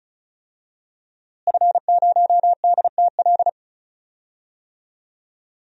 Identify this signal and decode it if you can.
Morse code, heard as a keyed tone, about 34 wpm, 700 Hz. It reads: F0DTL
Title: DXCC-100 Call Signs - 5 Characters - Repeated 3x - 35wpm